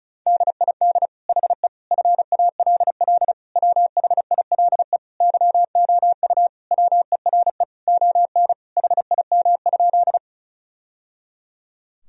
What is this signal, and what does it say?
Morse code, heard as a keyed tone, about 34 wpm, 700 Hz. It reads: DID HE FALL WHILE YOU WERE ON HIM?
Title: Sentences from Top 1000 Words - Review Part 2 - 35wpm